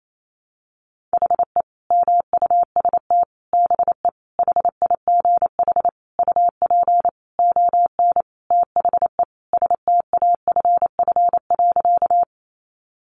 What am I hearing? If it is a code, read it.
Morse code, heard as a keyed tone, about 28 wpm, 700 Hz. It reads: HE MUST BE HIGH UP ON THE STAFF.
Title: Sentences from Top 1200 Words 28wpm